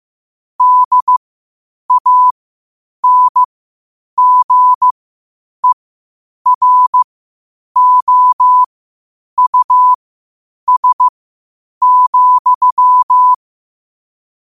Morse code, heard as a keyed tone, about 15 wpm, spaced as for 8 wpm, 1000 Hz. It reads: DANGEROUS,
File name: qotdfiles / 8wpm